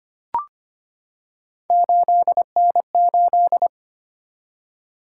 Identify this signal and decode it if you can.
Morse code, heard as a keyed tone, about 25 wpm, 700 Hz. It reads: 8N8
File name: Three Letter-Number Combinations - 25wpm